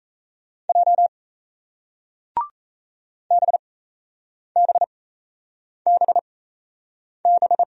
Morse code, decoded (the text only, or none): J BBBB